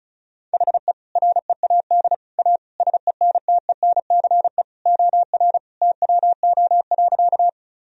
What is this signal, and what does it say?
Morse code, heard as a keyed tone, about 36 wpm, 700 Hz. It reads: HE READ A SENTENCE OR TWO.
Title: Sentences from Top 1000 Words - Review Part 1 - 35wpm